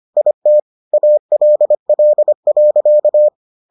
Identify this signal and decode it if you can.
Morse code, heard as a keyed tone, about 25 wpm, 600 Hz. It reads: IT ALL.